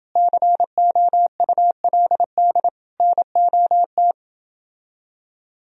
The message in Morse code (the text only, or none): COULD NOT